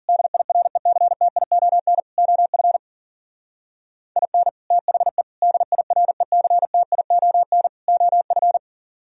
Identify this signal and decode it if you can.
Morse code, heard as a keyed tone, about 49 wpm, 700 Hz. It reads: DIRECTION OF IN THE DIRECTION OF
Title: Speed-Racing - Sets of 4 Words 40wpm